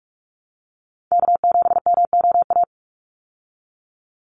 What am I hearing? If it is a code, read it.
Morse code, heard as a keyed tone, about 44 wpm, 700 Hz. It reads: K7MOA
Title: Call Signs Repeated 3x - 45wpm